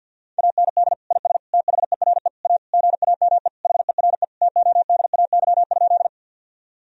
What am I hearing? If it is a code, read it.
Morse code, heard as a keyed tone, about 46 wpm, 700 Hz. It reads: AND IS THERE A GAME HERE TODAY?
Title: Sentences from Top 1000 Words - Review Part 3 - 50wpm